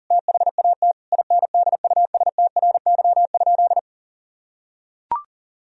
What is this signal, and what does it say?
Morse code, heard as a keyed tone, about 40 wpm, 700 Hz. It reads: THAT INDUSTRY?
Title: Sentences from Top 1000 Words - Review Part 4 - 40wpm